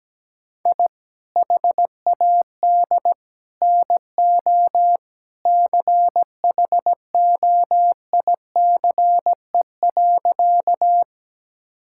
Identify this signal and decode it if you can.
Morse code, heard as a keyed tone, about 17 wpm, 700 Hz. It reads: I HAD NO CHOICE.
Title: Sentences from Top 1000 Words - Review Part 6 - 17wpm